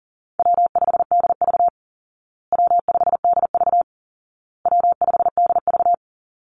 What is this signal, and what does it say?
Morse code, heard as a keyed tone, about 40 wpm, 700 Hz. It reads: W5DV W5DV W5DV
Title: Easy Call Signs Repeated 3x - 40wpm